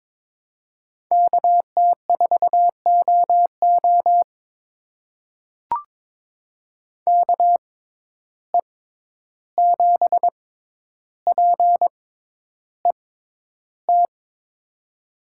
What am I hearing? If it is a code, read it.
Morse code, heard as a keyed tone, about 22 wpm, spaced as for 11 wpm, 700 Hz. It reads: KT4OO KE7PET
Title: USA Call Signs - 6x Character Spacing - 22wpm